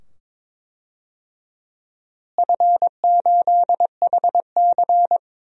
Morse code, heard as a keyed tone, about 21 wpm, 700 Hz. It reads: F8HC
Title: DXCC-100 Call Signs - 4 Characters - Repeated 2x - 22wpm